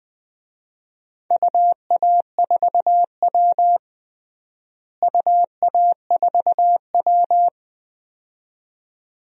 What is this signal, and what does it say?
Morse code, heard as a keyed tone, about 20 wpm, 700 Hz. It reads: UA4W UA4W
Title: DXCC-100 Call Signs - 4 Characters - Repeated 2x - 20wpm